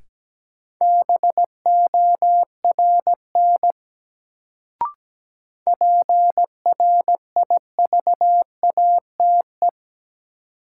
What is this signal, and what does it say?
Morse code, heard as a keyed tone, about 17 wpm, 700 Hz. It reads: BORN PRIVATE
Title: Sentences from Top 1000 Words - New Word Drill - 17wpm